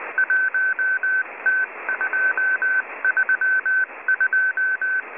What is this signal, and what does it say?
Morse code, heard as a keyed tone, about 19 wpm, 1550 Hz. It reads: 1T232